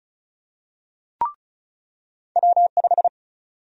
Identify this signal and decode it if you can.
Morse code, heard as a keyed tone, about 34 wpm, 700 Hz. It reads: W5
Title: USA Call Sign Prefixes - 35wpm